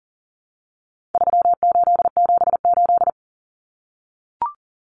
Morse code, heard as a keyed tone, about 40 wpm, 700 Hz. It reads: 3878
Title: Numbers - 4-Digit 40wpm